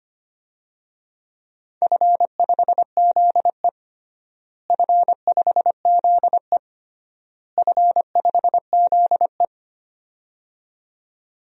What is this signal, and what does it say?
Morse code, heard as a keyed tone, about 25 wpm, 700 Hz. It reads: F5ZE F5ZE F5ZE
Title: DXCC-100 Call Signs - 4 Characters - Repeated 3x - 25wpm